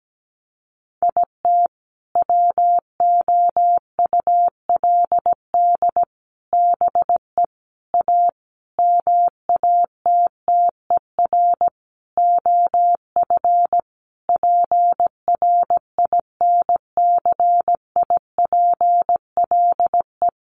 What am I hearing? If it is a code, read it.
Morse code, heard as a keyed tone, about 17 wpm, 700 Hz. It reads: IT WOULD BE A MATTER OF PRINCIPLE
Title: Sentences from Top 1400 Words 17wpm